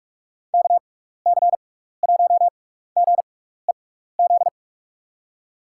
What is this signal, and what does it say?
Morse code, heard as a keyed tone, about 46 wpm, spaced as for 16 wpm, 700 Hz. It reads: KC1GEZ
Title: USA Call Signs - 6x Character Spacing - 45wpm